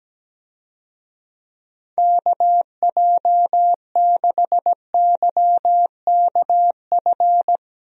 Morse code, heard as a keyed tone, about 17 wpm, 700 Hz. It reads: KJ6YKF
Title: USA Call Signs - 6 Characters - Repeated 2x - 17wpm